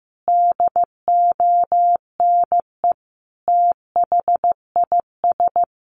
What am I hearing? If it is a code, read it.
Morse code, heard as a keyed tone, about 15 wpm, 700 Hz. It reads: DONE THIS
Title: Sentences from Top 1500 Words 15wpm